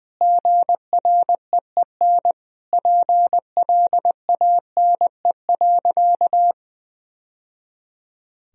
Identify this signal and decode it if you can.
Morse code, heard as a keyed tone, about 21 wpm, 700 Hz. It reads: GREEN PLANE.
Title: Sentences from Top 1000 Words - Review Part 3 - 20wpm